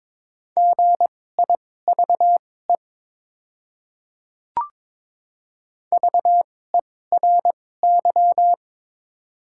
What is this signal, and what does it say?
Morse code, heard as a keyed tone, about 22 wpm, spaced as for 16 wpm, 690 Hz. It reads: GIVE VERY
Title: 2x Character Spacing - Top 200 Words - 22wpm